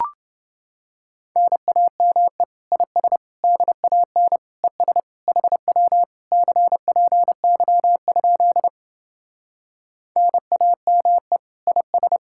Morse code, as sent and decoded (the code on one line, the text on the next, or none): -. .- -- . / .. ... / -.. .- -. / . ... / .... .-- / -.-. .--. -.-- ..--.. / -. .- -- . / .. ...
NAME IS DAN ES HW CPY? NAME IS